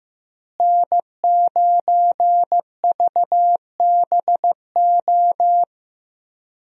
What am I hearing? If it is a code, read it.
Morse code, heard as a keyed tone, about 15 wpm, 700 Hz. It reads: N9VBO
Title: USA Call Signs - 5 Characters - Repeated 3x - 15wpm